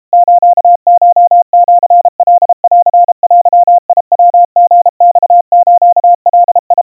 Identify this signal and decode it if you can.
Morse code (Japanese Wabun code), heard as a keyed tone, about 32 wpm, 700 Hz. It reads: スコシカンデヤリマスガ
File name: kawa / Morse sample / kamyu 120 cpm 700 Hz Crispy